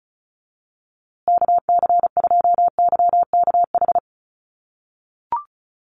Morse code, heard as a keyed tone, about 34 wpm, 700 Hz. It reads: KC2YKH
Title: USA Call Signs Repeated 2x 35wpm